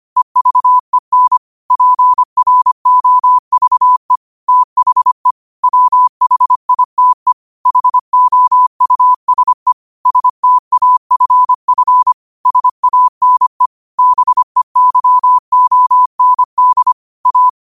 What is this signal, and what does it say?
Morse code, heard as a keyed tone, about 25 wpm, 1000 Hz. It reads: EVEN PROVE THE WHITE HOUSE STAFF SANE BEYOND A
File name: qotdfiles / 25wpm